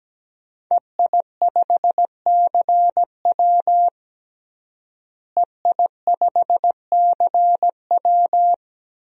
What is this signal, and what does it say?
Morse code, heard as a keyed tone, about 17 wpm, 700 Hz. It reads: EI5CW EI5CW
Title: DXCC-100 Call Signs - 5 Characters - Repeated 2x - 17wpm